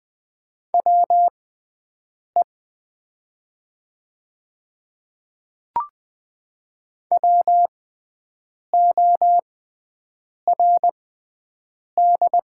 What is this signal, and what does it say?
Morse code, heard as a keyed tone, about 20 wpm, spaced as for 7 wpm, 700 Hz. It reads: WE WORD
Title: 6x Character Spacing - Top 100 Words - 20wpm